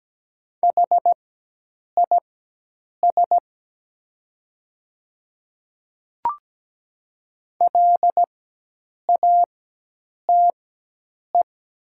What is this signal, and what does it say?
Morse code, heard as a keyed tone, about 17 wpm, spaced as for 8 wpm, 700 Hz. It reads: HIS LATE
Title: 4x Character Spacing - Top 300 Words - 17wpm